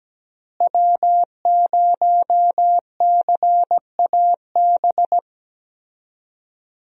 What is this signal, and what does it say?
Morse code, heard as a keyed tone, about 17 wpm, 700 Hz. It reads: W0CAB
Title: USA Call Signs - 5 Characters - Repeated 3x - 17wpm